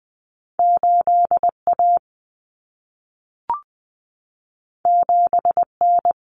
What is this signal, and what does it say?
Morse code, heard as a keyed tone, about 20 wpm, 700 Hz. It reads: 8A 7N
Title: Common Cut Numbers - 2-Digit 20wpm